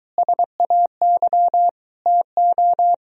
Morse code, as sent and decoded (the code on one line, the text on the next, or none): ... .- -.-- / - ---
SAY TO